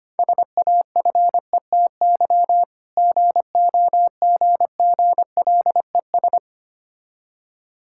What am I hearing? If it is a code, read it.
Morse code, heard as a keyed tone, about 25 wpm, 700 Hz. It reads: SAFETY GOGGLES
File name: practice / 25wpm / Difficult Words with Hints - 25wpm